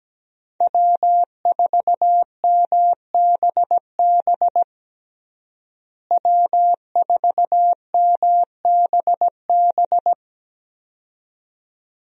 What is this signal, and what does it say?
Morse code, heard as a keyed tone, about 17 wpm, 700 Hz. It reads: W4MBB W4MBB
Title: USA Call Signs - 5 Characters - Repeated 2x - 17wpm